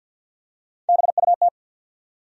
Morse code, heard as a keyed tone, about 53 wpm, 700 Hz. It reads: BUT
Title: Top 100 Words 50wpm